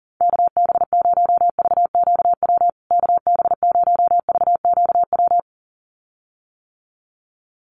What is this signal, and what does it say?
Morse code, heard as a keyed tone, about 40 wpm, 700 Hz. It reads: KB0VQW KB0VQW
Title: USA Call Signs Repeated 2x 40wpm